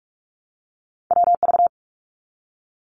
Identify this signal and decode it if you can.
Morse code, heard as a keyed tone, about 44 wpm, 700 Hz. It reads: WV
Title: ARRL Sections 45wpm